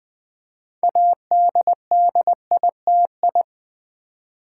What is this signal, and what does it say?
Morse code, heard as a keyed tone, about 20 wpm, 700 Hz. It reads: ADDITI
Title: Build A Word - 6 Letters At A Time - 12-Letter Words - 20wpm